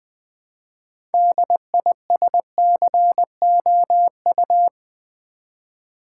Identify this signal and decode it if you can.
Morse code, heard as a keyed tone, about 20 wpm, 700 Hz. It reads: DISCOU
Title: Build A Word - 2 Letters At A Time - 8-Letter Words - 20wpm